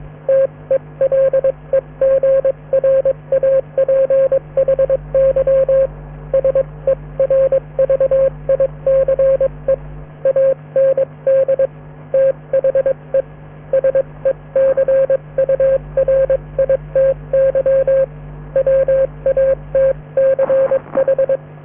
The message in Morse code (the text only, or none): TELEGRAPHY SERVICE AND THE SECURITY WATCH